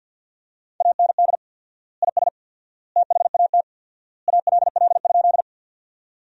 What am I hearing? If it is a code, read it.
Morse code, heard as a keyed tone, about 50 wpm, 700 Hz. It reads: AND IS THAT ALL?
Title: 4x Word-Spacing - Sentences from Top 100 Words - 50wpm